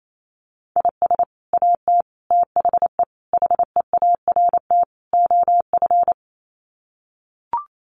Morse code, heard as a keyed tone, about 28 wpm, 700 Hz. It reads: IS AT THE HEART OF
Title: Sets of 5 Words Encore2 28wpm